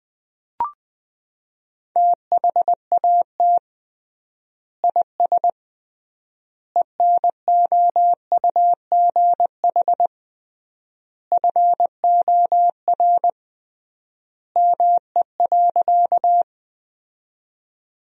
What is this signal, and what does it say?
Morse code, heard as a keyed tone, about 21 wpm, 700 Hz. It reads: THAT IS ENOUGH FOR ME.
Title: 3x Word-Spacing - Sentences from Top 400 Words - 20wpm